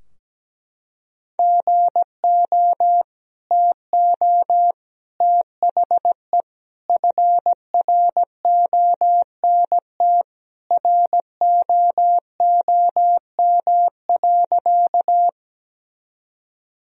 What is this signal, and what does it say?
Morse code, heard as a keyed tone, about 17 wpm, 700 Hz. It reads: GO TO THE FRONT ROOM.
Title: Sentences from Top 1000 Words - Review Part 2 - 17wpm